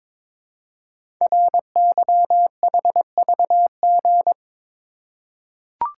Morse code, heard as a keyed tone, about 22 wpm, 700 Hz. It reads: RYHVG
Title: Code Groups - 5 Letters - 22wpm